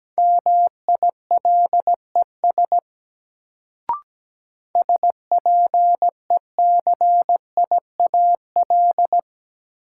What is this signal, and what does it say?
Morse code, heard as a keyed tone, about 17 wpm, 700 Hz. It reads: MILES SPECIAL
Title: Sentences from Top 1000 Words - Review Part 3 - New Word Drill - 17wpm